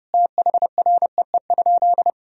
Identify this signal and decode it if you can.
Morse code, heard as a keyed tone, about 32 wpm, 700 Hz. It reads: THREE?
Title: Sentences from Top 1000 Words - Review Part 2 - 30wpm